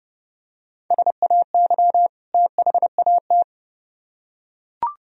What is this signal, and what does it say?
Morse code, heard as a keyed tone, about 28 wpm, 700 Hz. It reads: SAY THAT